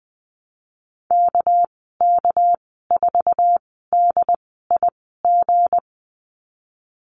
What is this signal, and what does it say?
Morse code, heard as a keyed tone, about 20 wpm, spaced as for 15 wpm, 700 Hz. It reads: KK4DIG
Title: 213 Lesson 34g - Number 8 - Call Signs 20wpm